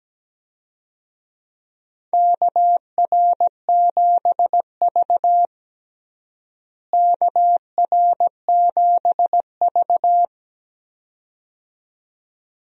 Morse code, heard as a keyed tone, about 17 wpm, 700 Hz. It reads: KR7V KR7V
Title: USA Call Signs - 4 Characters - Repeated 2x - 17wpm